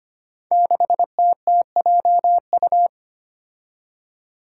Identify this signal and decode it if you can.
Morse code, heard as a keyed tone, about 25 wpm, 700 Hz. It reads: 6TTJU
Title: Five Challenging Letter-Number Combinations 25wpm